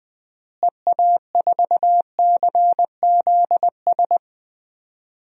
Morse code, heard as a keyed tone, about 20 wpm, 700 Hz. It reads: EA4CZS
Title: DXCC-100 Call Signs - 6 Characters - Repeated 3x - 20wpm